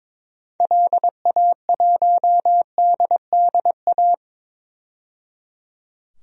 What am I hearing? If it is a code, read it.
Morse code, heard as a keyed tone, about 22 wpm, 700 Hz. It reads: LA1DDA
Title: DXCC-100 Call Signs - 6 Characters - 22wpm